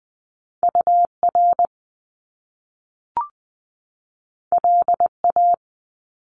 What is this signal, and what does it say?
Morse code, heard as a keyed tone, about 20 wpm, 700 Hz. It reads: U R L A
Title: Top Two Letter Combinations in Words 20wpm